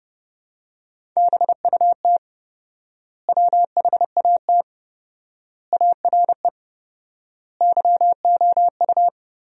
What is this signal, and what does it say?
Morse code, heard as a keyed tone, about 30 wpm, 700 Hz. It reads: BUT WHAT ARE YOU
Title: 4x Word-Spacing - Sentences from Top 100 Words - 30wpm